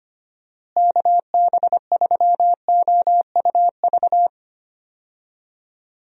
Morse code, heard as a keyed tone, about 25 wpm, 700 Hz. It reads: KB3OUV
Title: USA Call Signs - 6 Characters - Repeated 2x - 25wpm